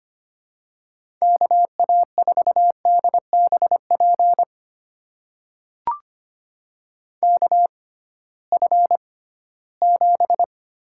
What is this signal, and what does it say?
Morse code, heard as a keyed tone, about 25 wpm, spaced as for 16 wpm, 700 Hz. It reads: KA4DBP KF7
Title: USA Call Signs - 6 Characters - 6x Character Spacing - 25wpm